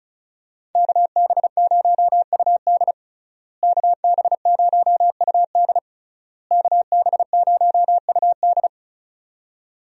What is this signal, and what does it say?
Morse code, heard as a keyed tone, about 36 wpm, 700 Hz. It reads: KB0UD KB0UD KB0UD
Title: USA Call Signs - 5 Characters - Repeated 3x - 35wpm